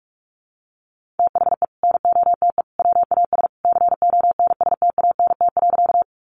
Morse code, heard as a keyed tone, about 44 wpm, 700 Hz. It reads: THE NOTE WAS CONSTANT.
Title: Sentences from Top 1500 Words 45wpm